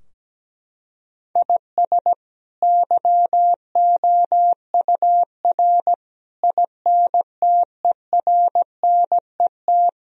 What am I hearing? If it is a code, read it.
Morse code, heard as a keyed tone, about 17 wpm, 700 Hz. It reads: IS YOUR INTERNET
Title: Sentences from Top 1000 Words - Review Part 4 - 17wpm